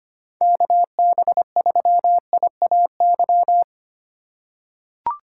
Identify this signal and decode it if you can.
Morse code, heard as a keyed tone, about 25 wpm, 700 Hz. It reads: KB3IAY
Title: USA Call Signs - 6 Characters - 3x Character Spacing - 25wpm